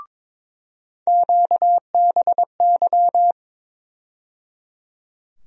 Morse code, heard as a keyed tone, about 21 wpm, 700 Hz. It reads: QBY
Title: USA Call Sign Postfixes - 3 Letters 22wpm